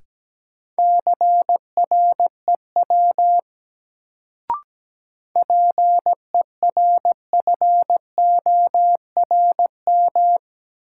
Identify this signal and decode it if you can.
Morse code, heard as a keyed tone, about 17 wpm, 700 Hz. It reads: CREW PERFORM
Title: Sentences from Top 1500 Words - New Word Drill Alt - 17wpm